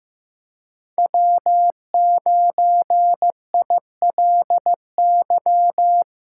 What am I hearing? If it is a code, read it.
Morse code, heard as a keyed tone, about 15 wpm, 700 Hz. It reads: W9ILY
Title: K1USN SST QSO Practice - 15wpm